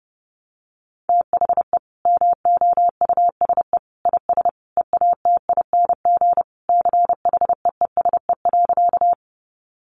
Morse code, heard as a keyed tone, about 28 wpm, 700 Hz. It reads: THE MOUSE IS EATING CHEESE.
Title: Easy Phrases 30wpm